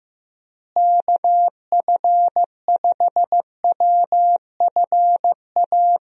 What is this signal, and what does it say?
Morse code, heard as a keyed tone, about 15 wpm, 700 Hz. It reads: KF5WFA
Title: USA Call Signs - 6 Characters - Repeated 2x - 15wpm